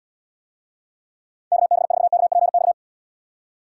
Morse code, heard as a keyed tone, about 78 wpm, 700 Hz. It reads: KB4KPX